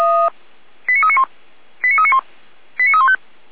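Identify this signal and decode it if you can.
Five-tone selective call (ZVEI), heard as quick runs of five tones, tones of about 70 ms. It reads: 89391 89391 89315
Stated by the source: ZVEI five-tone selective call received with an SDR.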